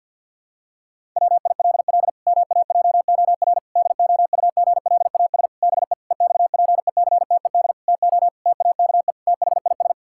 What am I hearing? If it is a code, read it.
Morse code, heard as a keyed tone, about 50 wpm, 700 Hz. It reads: WILL MAJOR DOUGLAS BE EXPECTED TO TAKE THIS
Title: Pangram Challenge v2 - 50wpm